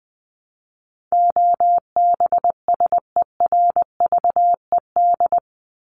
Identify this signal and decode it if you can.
Morse code, heard as a keyed tone, about 20 wpm, 700 Hz. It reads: OBSERVED
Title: ED Suffix - Top 100 Words with Base Words 20wpm